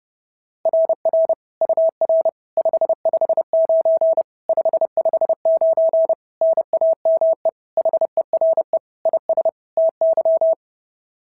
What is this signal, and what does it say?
Morse code, heard as a keyed tone, about 30 wpm, 690 Hz. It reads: RR UR 559 559 NAME HERE IS TY
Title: SOTA QSO Practice 30wpm